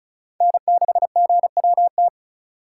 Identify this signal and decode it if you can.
Morse code, heard as a keyed tone, about 36 wpm, 700 Hz. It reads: N6GWT
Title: USA Call Signs - 5 Characters - 3x Character Spacing - 35wpm